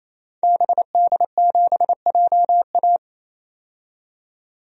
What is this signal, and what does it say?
Morse code, heard as a keyed tone, about 28 wpm, 700 Hz. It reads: BD7JA
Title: DXCC-100 Call Signs - 5 Characters - 28wpm